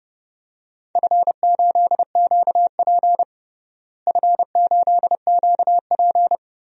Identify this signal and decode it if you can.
Morse code, heard as a keyed tone, about 30 wpm, 700 Hz. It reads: F8QP F8QP
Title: DXCC-100 Call Signs - 4 Characters - Repeated 3x - 30wpm